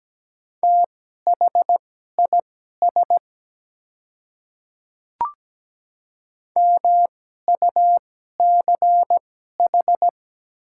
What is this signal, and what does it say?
Morse code, heard as a keyed tone, about 17 wpm, spaced as for 12 wpm, 700 Hz. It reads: THIS MUCH
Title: 2x Character Spacing - Top 200 Words - 17wpm